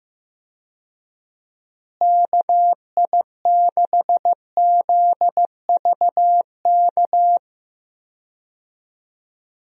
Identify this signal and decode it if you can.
Morse code, heard as a keyed tone, about 16 wpm, 700 Hz. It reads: KI6ZVK
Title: USA Call Signs - 6 Characters - Repeated 3x - 15wpm